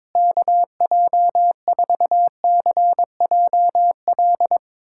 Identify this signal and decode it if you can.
Morse code, heard as a keyed tone, about 22 wpm, 700 Hz. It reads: KJ4CJL
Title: USA Callsign DE Callsign - 6 Characters - 22wpm